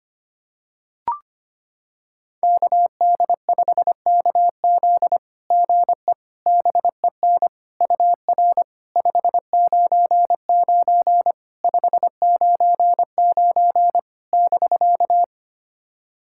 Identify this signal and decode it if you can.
Morse code, heard as a keyed tone, about 25 wpm, 700 Hz. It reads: KD5KZ GE BEN UR 599 599 <BK>
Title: SOTA QSO Practice 25wpm